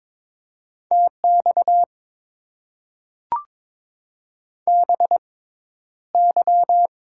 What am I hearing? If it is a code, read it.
Morse code, heard as a keyed tone, about 22 wpm, spaced as for 11 wpm, 700 Hz. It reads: TX BY
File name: Top 90 QSO Elements - 6x Character Spacing - 22wpm